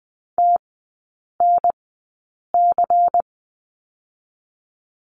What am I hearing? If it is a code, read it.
Morse code, heard as a keyed tone, about 20 wpm, spaced as for 8 wpm, 700 Hz. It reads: TNC
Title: 093 Lesson 16e - Letter c - Three Characters 20wpm